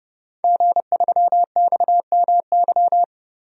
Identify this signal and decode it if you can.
Morse code, heard as a keyed tone, about 30 wpm, 700 Hz. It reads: G3XMY
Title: DXCC-100 Call Signs - 5 Characters - Repeated 2x - 30wpm